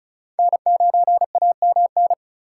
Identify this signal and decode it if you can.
Morse code, heard as a keyed tone, about 36 wpm, 700 Hz. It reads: N9AMN
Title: USA Call Signs - 5 Characters - 4x Character Spacing - 35wpm